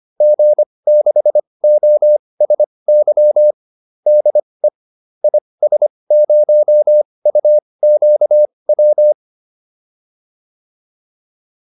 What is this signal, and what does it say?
Morse code, heard as a keyed tone, about 25 wpm, spaced as for 20 wpm, 600 Hz. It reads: G6OSY DE IS0UQW